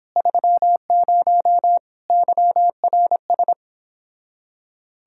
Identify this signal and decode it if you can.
Morse code, heard as a keyed tone, about 31 wpm, 700 Hz. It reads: 30 YRS